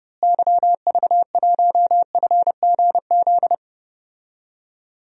Morse code, heard as a keyed tone, about 29 wpm, 700 Hz. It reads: YV1FGZ
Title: DXCC-100 Call Signs - 6 Characters - Repeated 3x - 30wpm